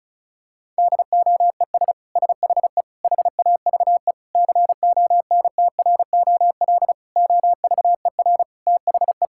Morse code, heard as a keyed tone, about 34 wpm, 700 Hz. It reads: DOES SHE HAVE CONTROL OVER THE
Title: Sentences from Top 1000 Words - Review Part 5 - 35wpm